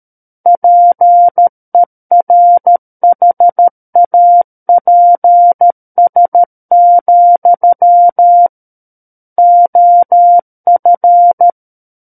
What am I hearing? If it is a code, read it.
Morse code, heard as a keyed tone, about 13 wpm, 700 Hz. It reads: PERHAPS, OF
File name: Book One / 04-Chapter 4 - 13 WPM